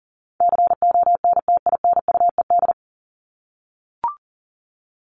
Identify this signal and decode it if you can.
Morse code, heard as a keyed tone, about 40 wpm, 700 Hz. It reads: CONTINUED
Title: ED Suffix - Top 100 Words with Base Words 40wpm